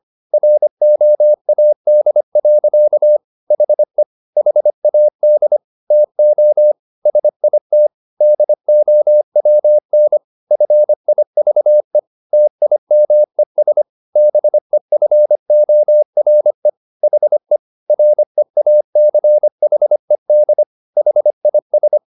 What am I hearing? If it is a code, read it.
Morse code, heard as a keyed tone, about 25 wpm, 600 Hz. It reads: ROAD. HE HAD TO SIT DOWN FIVE TIMES BEFORE HE REACHED HIS